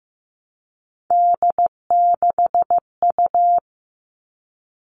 Morse code, heard as a keyed tone, about 15 wpm, 700 Hz. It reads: D6U
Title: All Cut Numbers - 3-Digit 15wpm